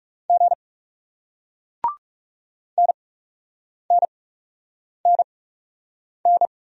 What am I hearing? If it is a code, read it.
Morse code, heard as a keyed tone, about 38 wpm, spaced as for 8 wpm, 700 Hz. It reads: G NNNN